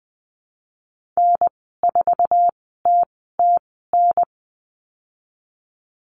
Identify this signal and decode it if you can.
Morse code, heard as a keyed tone, about 20 wpm, spaced as for 15 wpm, 700 Hz. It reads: N4TTN
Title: 048 Lesson 9g - Number 4 - Call Signs 20wpm